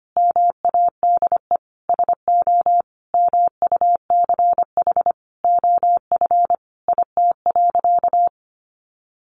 Morse code, heard as a keyed tone, about 25 wpm, 700 Hz. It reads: MADE SO MUCH OF IT.
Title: Sentences from Top 1400 Words 25wpm